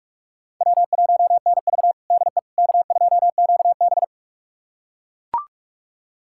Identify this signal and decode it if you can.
Morse code, heard as a keyed tone, about 41 wpm, 700 Hz. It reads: W1NV DE K2QB
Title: USA Callsign DE Callsign - 4 Characters - 45wpm